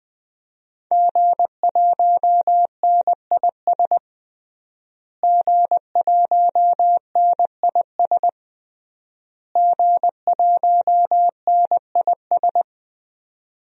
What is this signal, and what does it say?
Morse code, heard as a keyed tone, about 20 wpm, 700 Hz. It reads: G1NIS G1NIS G1NIS
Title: DXCC-100 Call Signs - 5 Characters - Repeated 3x - 20wpm